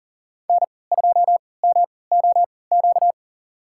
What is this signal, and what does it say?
Morse code, heard as a keyed tone, about 38 wpm, spaced as for 22 wpm, 700 Hz. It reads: N2MOQ
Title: USA Call Signs - 5 Characters - 3x Character Spacing - 40wpm